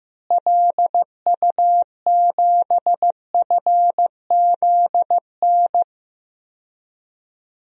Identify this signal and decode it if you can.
Morse code, heard as a keyed tone, about 15 wpm, 700 Hz. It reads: LU7FZN
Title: DXCC-100 Call Signs - 6 Characters - Repeated 2x - 15wpm